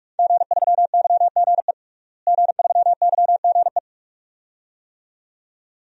Morse code, heard as a keyed tone, about 45 wpm, 700 Hz. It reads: G3YGE G3YGE
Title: DXCC-100 Call Signs - 5 Characters - Repeated 2x - 45wpm